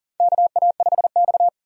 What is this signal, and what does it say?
Morse code, heard as a keyed tone, about 40 wpm, 700 Hz. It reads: KA5X
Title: USA Call Signs - 4 Characters - 40wpm